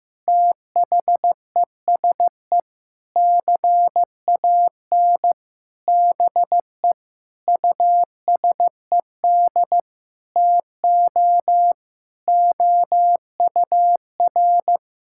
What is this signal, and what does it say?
Morse code, heard as a keyed tone, about 15 wpm, 700 Hz. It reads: THESE CAN BE USED TO OUR
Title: Sentences from Top 1000 Words - Review Part 6 - 15wpm